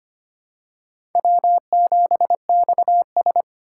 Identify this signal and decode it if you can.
Morse code, heard as a keyed tone, about 25 wpm, 700 Hz. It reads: W7XS
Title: USA Call Signs - 4x Character Spacing - 25wpm